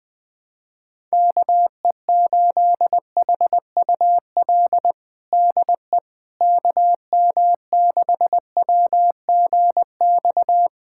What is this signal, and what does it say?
Morse code, heard as a keyed tone, about 20 wpm, 700 Hz. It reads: KE8HUL DE KM6WGX
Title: USA Callsign DE Callsign 20wpm